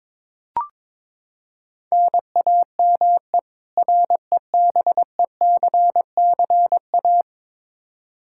Morse code, heard as a keyed tone, about 22 wpm, 700 Hz. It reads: NAME REBECCA
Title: K1USN SST QSO Practice 22wpm